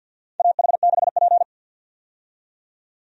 Morse code, heard as a keyed tone, about 50 wpm, 700 Hz. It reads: AH6P
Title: USA Call Signs - 4 Characters - 50wpm